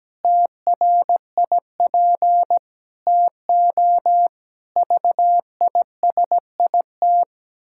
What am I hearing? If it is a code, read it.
Morse code, heard as a keyed tone, about 17 wpm, 700 Hz. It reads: TRIP TO VISIT
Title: Pangram Challenge v2 - 17wpm